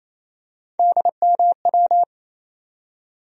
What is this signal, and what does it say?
Morse code, heard as a keyed tone, about 28 wpm, 700 Hz. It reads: DMW
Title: USA Call Sign Postfixes - 3 Letters 28wpm